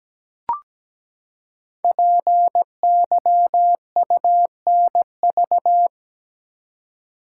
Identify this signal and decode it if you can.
Morse code, heard as a keyed tone, about 17 wpm, 700 Hz. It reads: PYUNV
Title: Code Groups - 5 Letters - 17wpm